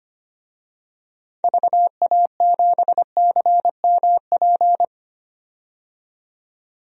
Morse code, heard as a keyed tone, about 25 wpm, 700 Hz. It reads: VA7CMP
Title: DXCC-100 Call Signs - 6 Characters - 25wpm